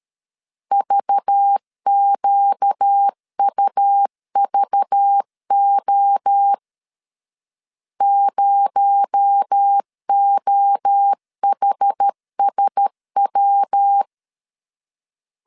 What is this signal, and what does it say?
Morse code, heard as a keyed tone, about 13 wpm, 800 Hz. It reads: VQUVO 0OHSW
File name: CW / DET / 1 6 1500C S B 12wpm 3431